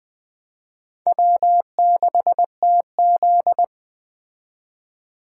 Morse code, heard as a keyed tone, about 20 wpm, 700 Hz. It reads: W6TZ